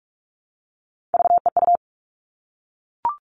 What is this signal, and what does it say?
Morse code, heard as a keyed tone, about 44 wpm, 700 Hz. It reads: VEU
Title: All Cut Numbers - 3-Digit 45wpm